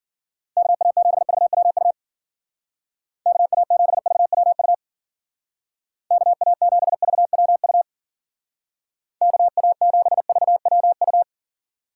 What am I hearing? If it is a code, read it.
Morse code, heard as a keyed tone, about 50 wpm, 700 Hz. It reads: KA7VWU KA7VWU KA7VWU KA7VWU